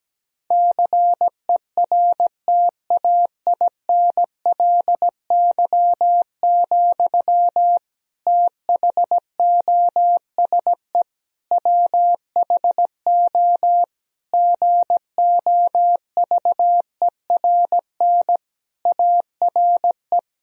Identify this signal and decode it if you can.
Morse code, heard as a keyed tone, about 17 wpm, 700 Hz. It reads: CERTAINLY, THOSE WHO GOVERN ARE
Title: Sentences from Top 1000 Words - Review Part 2 - 17wpm